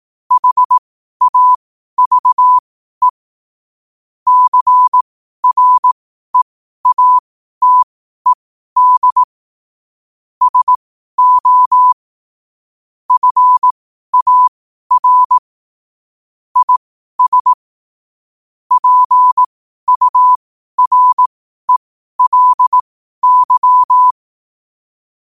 Morse code, heard as a keyed tone, about 18 wpm, spaced as for 12 wpm, 1000 Hz. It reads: HAVE CREATED SO FAR IS PURELY
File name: qotdfiles / 12wpm